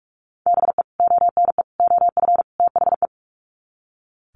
Morse code, heard as a keyed tone, about 44 wpm, 700 Hz. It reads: BE ONE OF THE
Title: Sets of 4 Words 45wpm